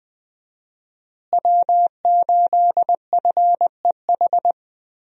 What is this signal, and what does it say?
Morse code, heard as a keyed tone, about 20 wpm, 700 Hz. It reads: W8FEH
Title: USA Call Signs - 5 Characters - Repeated 2x - 20wpm